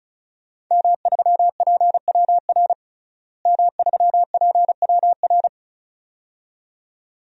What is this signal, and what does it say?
Morse code, heard as a keyed tone, about 36 wpm, 700 Hz. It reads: M3PWR M3PWR
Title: DXCC-100 Call Signs - 5 Characters - Repeated 2x - 35wpm